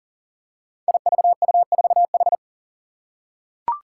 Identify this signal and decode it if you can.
Morse code, heard as a keyed tone, about 40 wpm, 700 Hz. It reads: IVU4H
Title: Five Challenging Letter-Number Combinations 40wpm